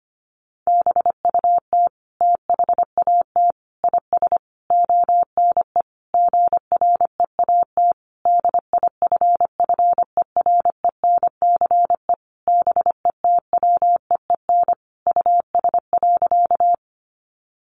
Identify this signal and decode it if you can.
Morse code, heard as a keyed tone, about 25 wpm, 700 Hz. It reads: BUT THAT IS ONE GREAT DIFFERENCE BETWEEN US.
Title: Sentences from Top 800 Words 25wpm